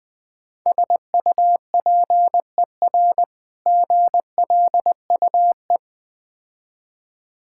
Morse code, heard as a keyed tone, about 20 wpm, 700 Hz. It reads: SUPER GLUE
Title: Difficult Words - Speed-Racing with Hints - 20wpm